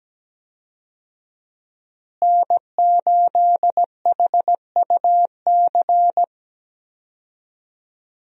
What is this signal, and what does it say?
Morse code, heard as a keyed tone, about 17 wpm, 700 Hz. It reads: N8HUC
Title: USA Call Signs - 5 Characters - Repeated 3x - 17wpm